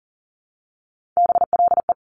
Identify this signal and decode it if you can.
Morse code, heard as a keyed tone, about 40 wpm, 700 Hz. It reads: BLE
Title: Top Three Letter Combinations in Words 40wpm